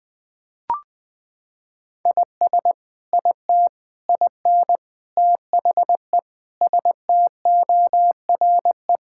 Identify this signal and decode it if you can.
Morse code, heard as a keyed tone, about 20 wpm, 700 Hz. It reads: IS IT IN THE STORE
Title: Sentences from Top 1000 Words - Review Part 5 - 20wpm